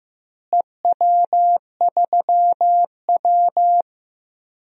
Morse code, heard as a keyed tone, about 15 wpm, 700 Hz. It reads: EW3W
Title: DXCC-100 Call Signs - 4 Characters - Repeated 2x - 15wpm